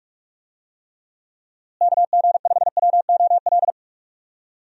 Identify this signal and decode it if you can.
Morse code, heard as a keyed tone, about 42 wpm, 700 Hz. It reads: KG5WOL
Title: USA Call Signs - 6 Characters - Repeated 3x - 45wpm